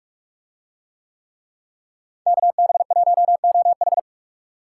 Morse code, heard as a keyed tone, about 48 wpm, 700 Hz. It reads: KB1OH